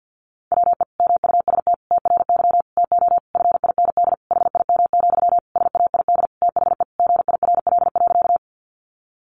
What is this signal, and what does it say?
Morse code, heard as a keyed tone, about 50 wpm, 700 Hz. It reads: WE MUST TRY TO FIND HIM, SAID THE GIRL.
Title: Sentences from Top 400 Words 50wpm